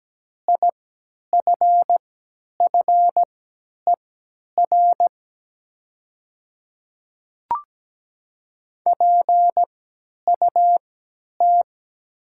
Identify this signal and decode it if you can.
Morse code, heard as a keyed tone, about 17 wpm, spaced as for 10 wpm, 700 Hz. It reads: IFFER PUT
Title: 3x Character Spacing - Top 200 Words - 17wpm